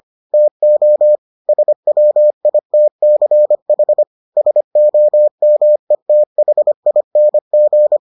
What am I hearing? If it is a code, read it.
Morse code, heard as a keyed tone, about 25 wpm, 600 Hz. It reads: TO SWITCH SOMETHING